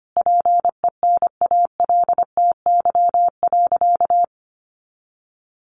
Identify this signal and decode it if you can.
Morse code, heard as a keyed tone, about 25 wpm, 700 Hz. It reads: PENALTY.
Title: A Sound Of Thunder 25wpm